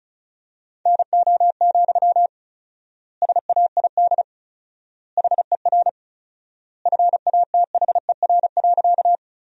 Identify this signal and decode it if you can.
Morse code, heard as a keyed tone, about 36 wpm, 700 Hz. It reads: NO, SAID HER FATHER.
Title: 4x Word-Spacing - Sentences from Top 300 Words - 35wpm